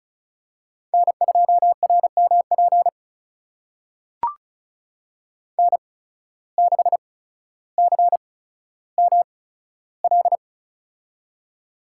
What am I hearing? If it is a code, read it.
Morse code, heard as a keyed tone, about 36 wpm, spaced as for 15 wpm, 700 Hz. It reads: N2RMP N6CML
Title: USA Call Signs - 8x Character Spacing - 35wpm